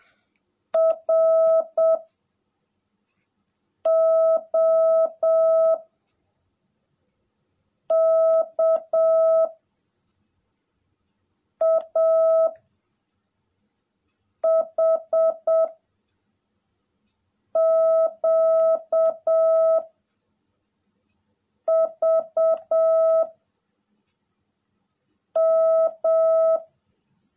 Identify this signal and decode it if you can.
Morse code, heard as a keyed tone, about 7 wpm, 650 Hz. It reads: R O K A H Q V M
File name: jh8cbh / 2021cw / cw18a